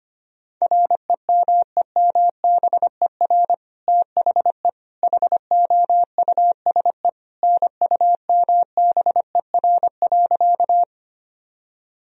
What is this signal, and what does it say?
Morse code, heard as a keyed tone, about 25 wpm, 700 Hz. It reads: REMEMBER THE HOUSE NUMBER.
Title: Sentences from Top 1000 Words - Review Part 1 - 25wpm